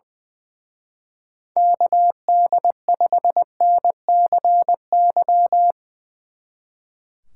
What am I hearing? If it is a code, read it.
Morse code, heard as a keyed tone, about 22 wpm, 700 Hz. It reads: KD5NCY
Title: Speed Racing - USA Call Signs 20wpm